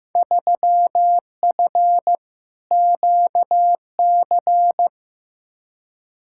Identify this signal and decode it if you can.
Morse code, heard as a keyed tone, about 16 wpm, 700 Hz. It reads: 3F QC
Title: Field Day Exchanges - 15wpm